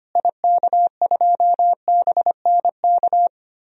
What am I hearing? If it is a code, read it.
Morse code, heard as a keyed tone, about 25 wpm, 700 Hz. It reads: IK2BNK
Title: DXCC-100 Call Signs - 6 Characters - Repeated 2x - 25wpm